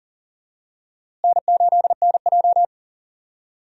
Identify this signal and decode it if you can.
Morse code, heard as a keyed tone, about 40 wpm, 700 Hz. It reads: N8NJ
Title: USA Call Signs - 4 Characters - Repeated 2x - 40wpm